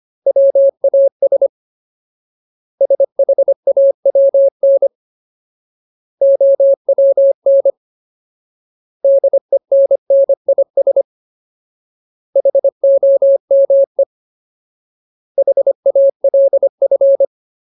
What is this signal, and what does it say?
Morse code, heard as a keyed tone, about 25 wpm, 550 Hz. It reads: WAS SHAWN OWN DENNIS HOME HALF
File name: Ultimate Word List / 25WPM Ultimate